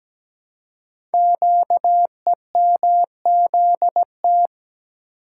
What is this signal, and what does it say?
Morse code, heard as a keyed tone, about 17 wpm, 700 Hz. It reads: QEMZT
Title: Code Groups - 5 Letters - 17wpm